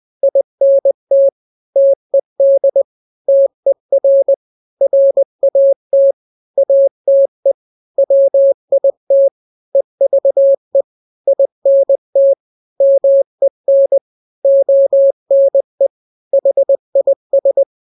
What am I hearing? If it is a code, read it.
Morse code, heard as a keyed tone, about 20 wpm, 550 Hz. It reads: INT TED TER RAT ATE WIT EVE INT MEN ONE HIS
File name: n-grams / 3-grams 20 18 WPM